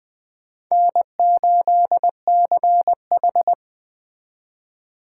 Morse code, heard as a keyed tone, about 20 wpm, 700 Hz. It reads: N8CH
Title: USA Call Signs - 4 Characters - 2x Character Spacing - 20wpm